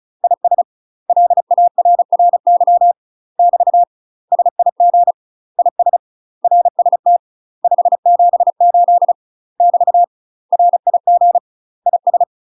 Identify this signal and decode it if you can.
Morse code, heard as a keyed tone, about 36 wpm, 700 Hz. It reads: IS LARRY <BT> SIG IS RST 578 <BT> RIG IS